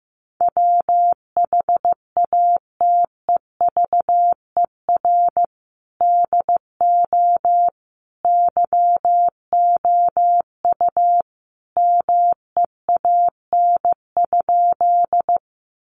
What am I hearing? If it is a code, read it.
Morse code, heard as a keyed tone, about 15 wpm, 700 Hz. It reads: WHATEVER DO YOU MEAN?
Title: Sentences from Top 1100 Words 15wpm